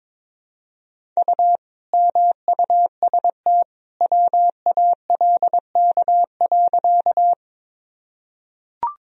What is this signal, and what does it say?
Morse code, heard as a keyed tone, about 22 wpm, 700 Hz. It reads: U MUST WALK.
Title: Sentences from Top 300 Words and Abbreviations 22wpm